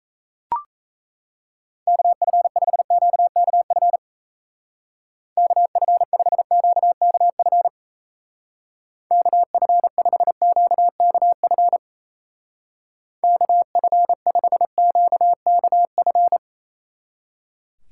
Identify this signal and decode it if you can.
Morse code, heard as a keyed tone, about 34 wpm, 700 Hz. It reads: KF5QKF KF5QKF KF5QKF KF5QKF
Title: Speed Racing - USA 2x3 Call Signs 28wpm